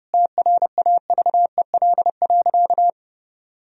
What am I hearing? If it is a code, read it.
Morse code, heard as a keyed tone, about 32 wpm, 700 Hz. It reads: TRAVEL.
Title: Sentences from Top 1000 Words - Review Part 4 - 30wpm